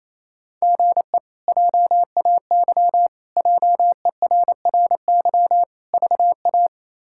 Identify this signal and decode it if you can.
Morse code, heard as a keyed tone, about 28 wpm, 700 Hz. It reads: GE JAY JERRY VA
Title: K1USN SST QSO Practice - 28wpm